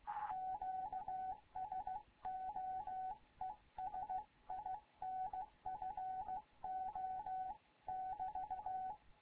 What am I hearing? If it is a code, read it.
Morse code, heard as a keyed tone, about 16 wpm, 750 Hz, with radio noise. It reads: QSO ES INFO <BT>
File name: Morze-HA2MN / hangp / 2 11 GYAKORLAT